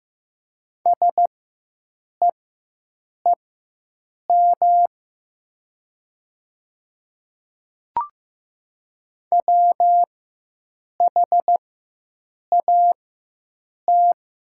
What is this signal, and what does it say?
Morse code, heard as a keyed tone, about 15 wpm, spaced as for 7 wpm, 700 Hz. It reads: SEEM WHAT
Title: 4x Character Spacing - Top 300 Words - 15wpm